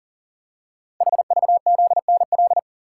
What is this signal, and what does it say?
Morse code, heard as a keyed tone, about 43 wpm, 700 Hz. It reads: HVZNL
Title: Code Groups - 5 Letters - 40wpm